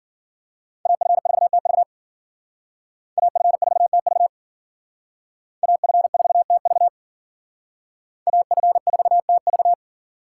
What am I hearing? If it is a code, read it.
Morse code, heard as a keyed tone, about 51 wpm, 700 Hz. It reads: AF4TV AF4TV AF4TV AF4TV